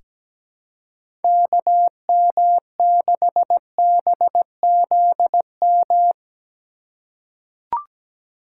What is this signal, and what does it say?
Morse code, heard as a keyed tone, about 17 wpm, 700 Hz. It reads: KM6BZM
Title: USA Call Signs - 6 Characters - 4x Character Spacing - 17wpm